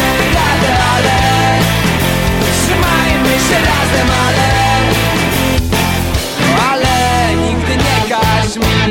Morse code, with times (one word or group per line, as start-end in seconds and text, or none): none